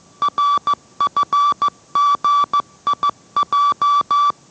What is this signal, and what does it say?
Morse code, heard as a keyed tone, about 15 wpm, 1160 Hz. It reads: RFGIJ